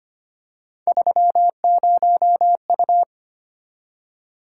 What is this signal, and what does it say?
Morse code, heard as a keyed tone, about 25 wpm, 700 Hz. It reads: 30U
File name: Three Letter-Number Combinations - 25wpm